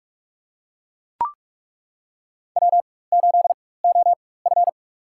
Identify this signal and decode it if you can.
Morse code, heard as a keyed tone, about 43 wpm, spaced as for 21 wpm, 700 Hz. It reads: W8OF
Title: USA Call Signs - 4 Characters - 4x Character Spacing - 45wpm